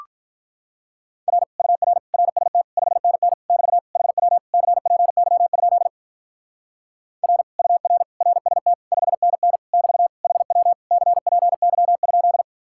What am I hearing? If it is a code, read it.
Morse code, heard as a keyed tone, about 44 wpm, 700 Hz. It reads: R UR RST 5NN <BT> HW CPY? R UR RST 5NN <BT> HW CPY?